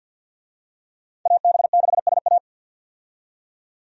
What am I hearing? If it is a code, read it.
Morse code, heard as a keyed tone, about 53 wpm, 700 Hz. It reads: AB6SA
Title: USA Call Signs - 5 Characters - 50wpm